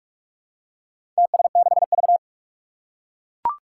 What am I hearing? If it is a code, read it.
Morse code, heard as a keyed tone, about 44 wpm, 700 Hz. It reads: TS6V
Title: Four Challenging Letter-Number Combinations 45wpm